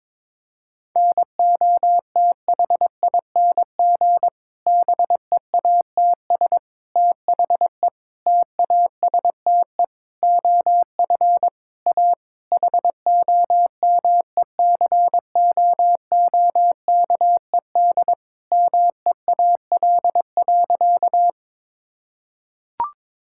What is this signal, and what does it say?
Morse code, heard as a keyed tone, about 22 wpm, 700 Hz. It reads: NOTHING BEATS THE TASTE OF A HOMECOOKED MEAL.
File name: Difficult Sentences with Hints - 22wpm